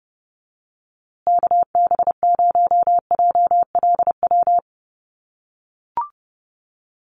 Morse code, heard as a keyed tone, about 30 wpm, 700 Hz. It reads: KB0JLW
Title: USA Call Signs Repeated 2x 30wpm